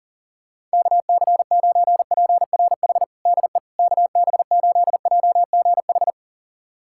Morse code, heard as a keyed tone, about 40 wpm, 700 Hz. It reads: KC9PRH DE KB8JGH
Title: USA Callsign DE Callsign - 6 Characters - 40wpm